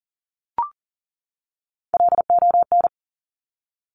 Morse code, heard as a keyed tone, about 39 wpm, 690 Hz. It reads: LON
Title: Top Three Letter Combinations in Words 40wpm